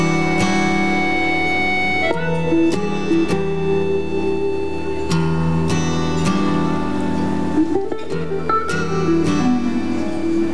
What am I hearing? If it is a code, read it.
Dial-up modem handshake.